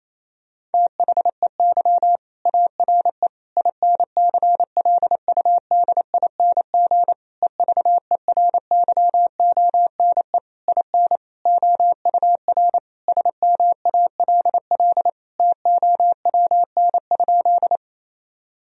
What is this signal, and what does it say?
Morse code, heard as a keyed tone, about 28 wpm, 700 Hz. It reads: THEY ARE INCLUDING EVERYONE IN OUR SMALL TOWN?
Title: Sentences from Top 1000 Words - Review Part 4 - 28wpm